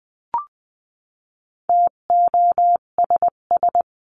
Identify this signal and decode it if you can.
Morse code, heard as a keyed tone, about 20 wpm, 700 Hz. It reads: TOSS
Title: 258 Lesson 42c - Words 16-20wpm